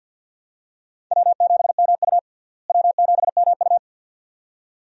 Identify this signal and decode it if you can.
Morse code, heard as a keyed tone, about 51 wpm, 700 Hz. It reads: W7MU W7MU
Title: USA Call Signs - 4 Characters - Repeated 2x - 50wpm